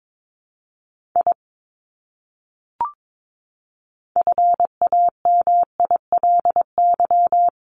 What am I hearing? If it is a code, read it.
Morse code, heard as a keyed tone, about 22 wpm, 700 Hz. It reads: I FAMILY
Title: Top 500 Words Repeated 22wpm